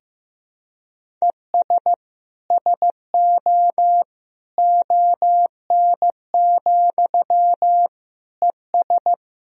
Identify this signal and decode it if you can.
Morse code, heard as a keyed tone, about 15 wpm, 700 Hz. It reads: ES SO ON, ES
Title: Sentences from Top 100 Words and Abbreviations 15wpm